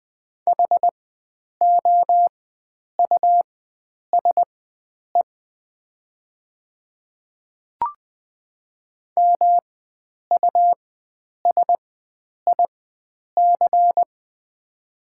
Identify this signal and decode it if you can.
Morse code, heard as a keyed tone, about 20 wpm, spaced as for 9 wpm, 700 Hz. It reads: HOUSE MUSIC
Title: 4x Character Spacing - Top 300 Words - 20wpm